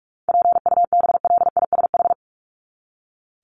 Morse code, heard as a keyed tone, about 44 wpm, 700 Hz. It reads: PUBLISH
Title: ED Suffix - Top 100 Words with Base Words 45wpm